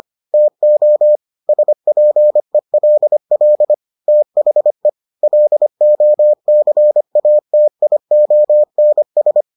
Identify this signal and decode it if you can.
Morse code, heard as a keyed tone, about 25 wpm, 600 Hz. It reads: TO SPELL THE LOCATIONS